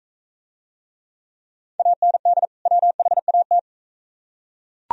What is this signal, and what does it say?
Morse code, heard as a keyed tone, about 38 wpm, 700 Hz. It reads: AND WHAT